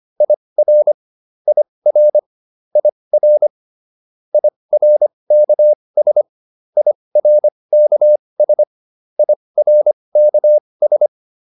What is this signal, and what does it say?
Morse code, heard as a keyed tone, about 25 wpm, spaced as for 20 wpm, 600 Hz. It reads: IR IR IR IRKS IRKS IRKS